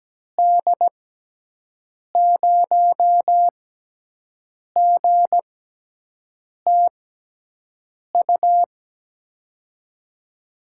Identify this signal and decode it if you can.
Morse code, heard as a keyed tone, about 17 wpm, spaced as for 6 wpm, 700 Hz. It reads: D0GTU
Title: USA Call Signs - 6 Characters - 6x Character Spacing - 17wpm